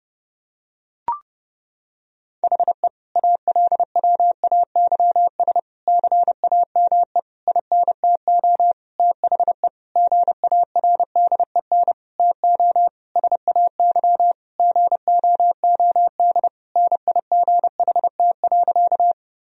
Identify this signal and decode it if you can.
Morse code, heard as a keyed tone, about 32 wpm, 700 Hz. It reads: HE ALWAYS CAME INTO THE GARDEN TO SAY GOOD NIGHT.
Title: Sentences from Top 1000 Words - Review Part 4 - 30wpm